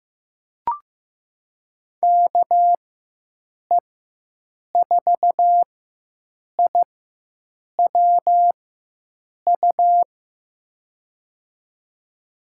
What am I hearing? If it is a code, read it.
Morse code, heard as a keyed tone, about 15 wpm, spaced as for 7 wpm, 700 Hz. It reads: KE4IWU
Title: USA Call Signs - 4x Character Spacing - 15wpm